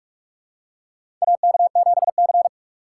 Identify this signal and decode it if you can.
Morse code, heard as a keyed tone, about 44 wpm, 700 Hz. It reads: AK7C
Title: USA Call Signs - 4 Characters - 6x Character Spacing - 45wpm